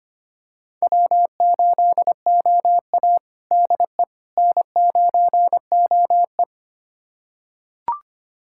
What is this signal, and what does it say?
Morse code, heard as a keyed tone, about 25 wpm, 700 Hz. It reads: W8OA DE N9OE
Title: USA Callsign DE Callsign - 4 Characters - 25wpm